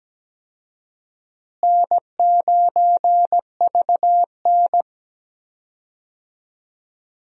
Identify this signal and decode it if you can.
Morse code, heard as a keyed tone, about 17 wpm, 700 Hz. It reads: N9VN
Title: USA Call Signs - 4 Characters - Repeated 3x - 17wpm